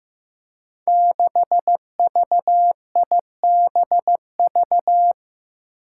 Morse code, heard as a keyed tone, about 15 wpm, 700 Hz. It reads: 6VIBV
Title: Five Challenging Letter-Number Combinations 15wpm